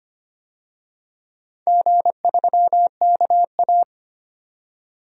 Morse code, heard as a keyed tone, about 25 wpm, 700 Hz. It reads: G3KA
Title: DXCC-100 Call Signs - 4 Characters - 25wpm